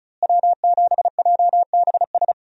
Morse code, heard as a keyed tone, about 36 wpm, 700 Hz. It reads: W7JBS
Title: USA Call Signs - 5 Characters - Repeated 2x - 35wpm